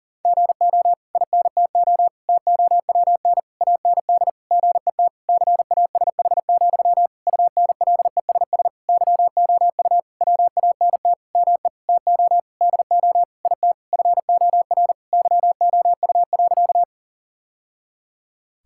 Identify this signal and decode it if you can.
Morse code, heard as a keyed tone, about 40 wpm, 700 Hz. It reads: GO INTO TOWN AND GET CASH, UNLESS YOU WANT ME TO DO IT FOR YOU.
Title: Sentences from Top 1000 Words - Review Part 6 - 40wpm